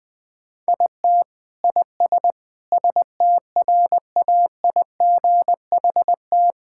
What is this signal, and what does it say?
Morse code, heard as a keyed tone, about 20 wpm, 700 Hz. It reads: IT IS STRAIGHT
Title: Sentences from Top 1000 Words - Review Part 6 - 20wpm